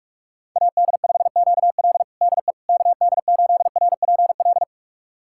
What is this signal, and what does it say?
Morse code, heard as a keyed tone, about 44 wpm, 700 Hz. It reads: AD5QL DE KD8RPL
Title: USA Callsign DE Callsign 45wpm